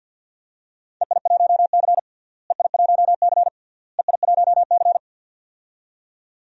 Morse code, heard as a keyed tone, about 52 wpm, 700 Hz. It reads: EI1C EI1C EI1C
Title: DXCC-100 Call Signs - 4 Characters - Repeated 3x - 50wpm